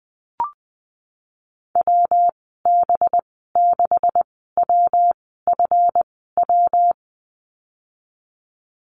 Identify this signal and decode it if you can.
Morse code, heard as a keyed tone, about 20 wpm, spaced as for 15 wpm, 700 Hz. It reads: WB6WFW
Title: 193 Lesson 31g - Letter v - Call Signs 20wpm